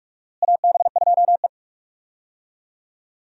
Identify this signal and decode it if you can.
Morse code, heard as a keyed tone, about 47 wpm, 700 Hz. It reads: AB2E
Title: USA Call Signs - 4 Characters - Repeated 2x - 45wpm